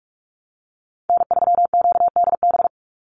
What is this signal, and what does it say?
Morse code, heard as a keyed tone, about 44 wpm, 700 Hz. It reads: N3QDB
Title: USA Call Signs 45wpm